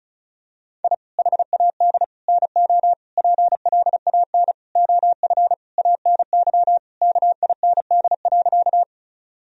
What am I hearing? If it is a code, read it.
Morse code, heard as a keyed tone, about 34 wpm, 700 Hz. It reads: I HAD NO PLAN OF ANY KIND.
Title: Sentences from Top 1000 Words - Review Part 2 - 35wpm